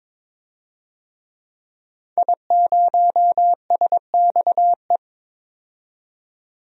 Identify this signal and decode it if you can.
Morse code, heard as a keyed tone, about 22 wpm, 700 Hz. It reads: I0SXE
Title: DXCC-100 Call Signs - 5 Characters - 22wpm